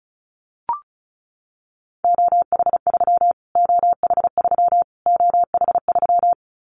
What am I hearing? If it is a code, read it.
Morse code, heard as a keyed tone, about 34 wpm, 700 Hz. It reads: OH3 OH3 OH3
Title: Callsign Prefixes for Top 25 DXCC - Repeated 3x - 35wpm